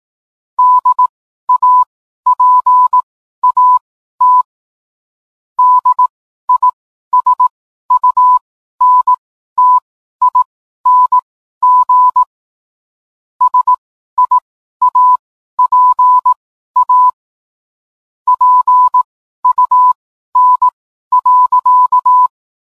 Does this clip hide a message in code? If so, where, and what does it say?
Yes: Morse code from 0.6 to 22.3 s, heard as a keyed tone, about 18 wpm, spaced as for 12 wpm, 1000 Hz. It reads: DAPAT DISUNTING SIAPA PUN.